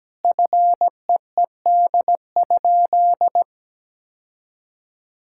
Morse code, heard as a keyed tone, about 17 wpm, 700 Hz. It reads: FEED?
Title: Sentences from Top 1600 Words 17wpm